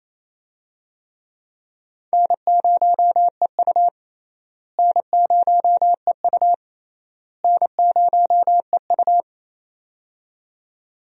Morse code, heard as a keyed tone, about 28 wpm, 700 Hz. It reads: N0EU N0EU N0EU
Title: USA Call Signs - 4 Characters - Repeated 3x - 28wpm